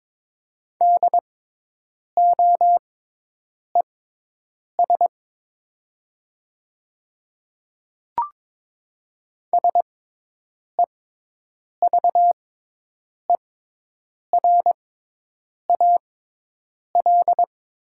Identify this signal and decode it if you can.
Morse code, heard as a keyed tone, about 22 wpm, spaced as for 8 wpm, 700 Hz. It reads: DOES SEVERAL
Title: 6x Character Spacing - Top 500 Words - 22wpm